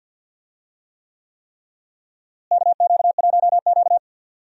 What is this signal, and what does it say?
Morse code, heard as a keyed tone, about 50 wpm, 700 Hz. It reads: KQ1Q
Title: USA Call Signs - 4 Characters - Repeated 3x - 50wpm